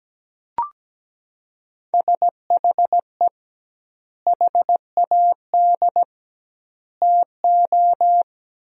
Morse code, heard as a keyed tone, about 17 wpm, 700 Hz. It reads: SHE HAD TO
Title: 2x Word-Spacing - Sentences from Top 200 Words - 17wpm